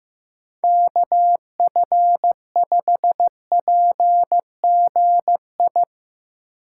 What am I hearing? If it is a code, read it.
Morse code, heard as a keyed tone, about 15 wpm, 700 Hz. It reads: KF5PGI
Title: USA Call Signs - 6 Characters - 8x Character Spacing - 15wpm